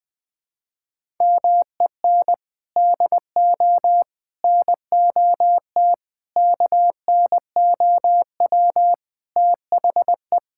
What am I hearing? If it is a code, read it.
Morse code, heard as a keyed tone, about 20 wpm, 700 Hz. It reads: MEN DO NOT KNOW THE
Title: Sentences from Top 1000 Words - Review Part 6 - 20wpm